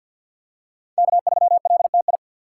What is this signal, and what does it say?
Morse code, heard as a keyed tone, about 52 wpm, 700 Hz. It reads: K3LTI
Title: USA Call Signs - 5 Characters - 6x Character Spacing - 50wpm